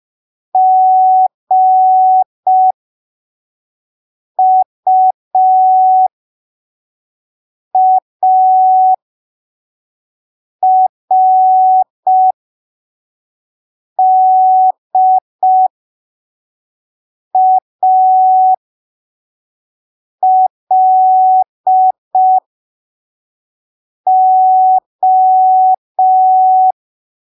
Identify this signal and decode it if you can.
Morse code, heard as a keyed tone, about 5 wpm, spaced as for 3 wpm, 750 Hz. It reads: GUARDALO
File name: arquivos / texto / texto 20 05ppm